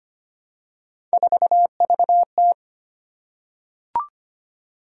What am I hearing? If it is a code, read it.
Morse code, heard as a keyed tone, about 25 wpm, 700 Hz. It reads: 4VT
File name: Three Letter-Number Combinations - 25wpm